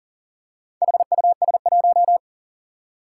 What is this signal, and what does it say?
Morse code, heard as a keyed tone, about 40 wpm, 700 Hz. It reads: HUS1
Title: Four Challenging Letter-Number Combinations 40wpm